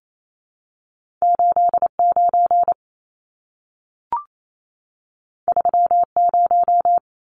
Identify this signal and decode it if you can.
Morse code, heard as a keyed tone, about 28 wpm, 700 Hz. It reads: 89 30
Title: Numbers - 2-Digit 28wpm